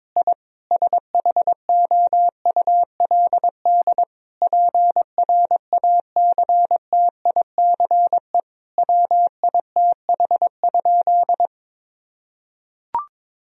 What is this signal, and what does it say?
Morse code, heard as a keyed tone, about 22 wpm, 700 Hz. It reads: I SHOULD PRACTICE WITH?
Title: Sentences from Top 1000 Words - Review Part 4 - 22wpm